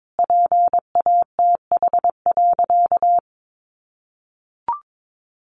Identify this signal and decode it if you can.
Morse code, heard as a keyed tone, about 22 wpm, 700 Hz. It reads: PATH.
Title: A Sound Of Thunder 22wpm